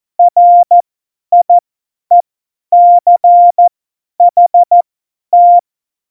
Morse code, heard as a keyed tone, about 14 wpm, spaced as for 10 wpm, 700 Hz. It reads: RIECHT
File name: Morsekurs / Witze / Witz 14. Januar 2020, 50 F70